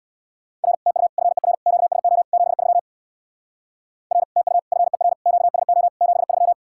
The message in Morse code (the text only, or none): R TU FER QSO 73 R TU FER QSO 73